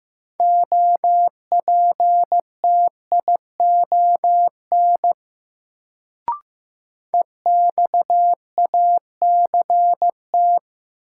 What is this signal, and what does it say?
Morse code, heard as a keyed tone, about 15 wpm, 700 Hz. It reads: OPTION EXACT
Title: Sentences from Top 1300 Words - New Word Drill Alt - 15wpm